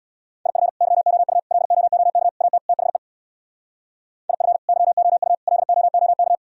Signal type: Morse code

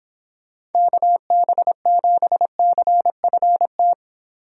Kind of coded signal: Morse code

